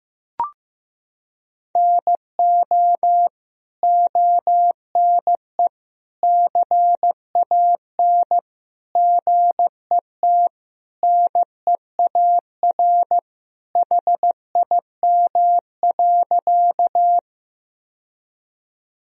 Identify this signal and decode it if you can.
Morse code, heard as a keyed tone, about 15 wpm, 700 Hz. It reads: NO ONE CAN GET NEAR HIM.
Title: Sentences from Top 1000 Words - Review Part 1 - 15wpm